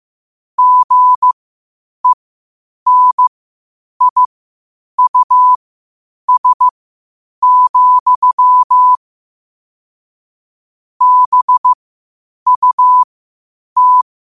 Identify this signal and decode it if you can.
Morse code, heard as a keyed tone, about 15 wpm, spaced as for 8 wpm, 1000 Hz. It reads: GENIUS, BUT
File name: qotdfiles / 8wpm